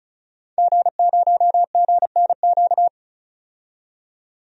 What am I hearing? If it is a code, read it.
Morse code, heard as a keyed tone, about 36 wpm, 700 Hz. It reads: G0GNQ
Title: DXCC-100 Call Signs - 5 Characters - 35wpm